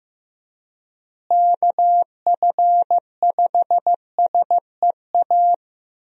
Morse code, heard as a keyed tone, about 16 wpm, 700 Hz. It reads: KF5SEA
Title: USA Call Signs - 6 Characters - Repeated 3x - 15wpm